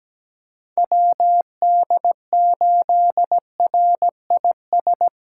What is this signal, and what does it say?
Morse code, heard as a keyed tone, about 17 wpm, 700 Hz. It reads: WD8RIS
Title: USA Callsign DE Callsign - 6 Characters - 17wpm